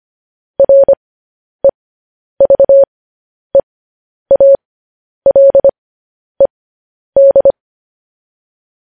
Morse code, heard as a keyed tone, about 25 wpm, spaced as for 10 wpm, 550 Hz. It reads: REVEALED